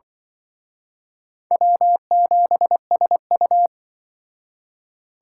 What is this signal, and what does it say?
Morse code, heard as a keyed tone, about 25 wpm, 700 Hz. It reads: W7SU
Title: Speed Racing - USA 1x2 Call Signs 20wpm